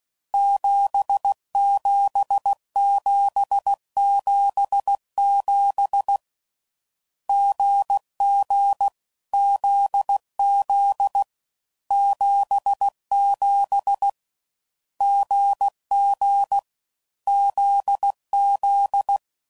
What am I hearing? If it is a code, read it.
Morse code, heard as a keyed tone, about 16 wpm, 800 Hz. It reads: 77777 GG ZZ 77 GG ZZ
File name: Combinations / Difficult Combos / GZ GZ7